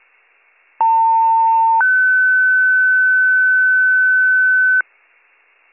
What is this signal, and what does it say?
Two-tone page, heard as a one-second tone, then a three-second tone, about 900 then 1500 Hz. It is heard from 0.8 to 4.8 s.